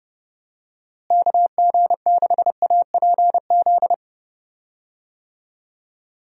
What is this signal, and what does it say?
Morse code, heard as a keyed tone, about 30 wpm, 700 Hz. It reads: KG6APZ
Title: USA Call Signs - 6 Characters - Repeated 3x - 30wpm